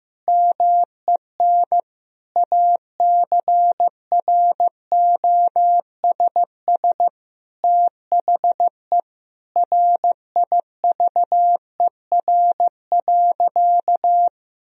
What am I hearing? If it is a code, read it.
Morse code, heard as a keyed tone, about 16 wpm, 700 Hz. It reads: MEN ACROSS THE RIVER.
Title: Sentences from Top 1000 Words - Review Part 2 - 15wpm